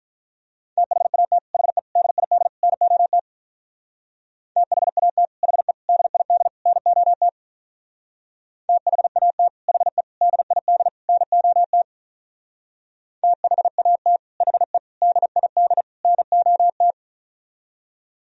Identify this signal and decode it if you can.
Morse code, heard as a keyed tone, about 46 wpm, 700 Hz. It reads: THAT HE DID NOT THAT HE DID NOT THAT HE DID NOT THAT HE DID NOT